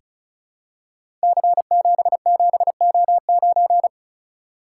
Morse code, heard as a keyed tone, about 36 wpm, 700 Hz. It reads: C77O9
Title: Code Groups - 5 Letter:Numbers - 35wpm